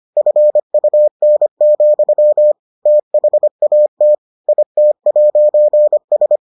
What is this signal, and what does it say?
Morse code, heard as a keyed tone, about 25 wpm, 600 Hz. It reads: FUN, THAT IT'S